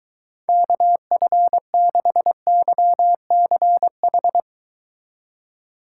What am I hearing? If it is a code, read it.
Morse code, heard as a keyed tone, about 23 wpm, 700 Hz. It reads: KF6YCH